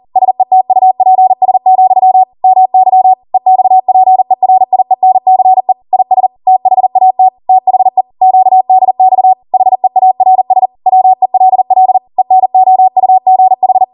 Morse code, heard as a keyed tone, about 40 wpm, 750 Hz. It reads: SETUPS, MY EXPERIENCE IS THAT THE QDX HEARS WELL ENOUGH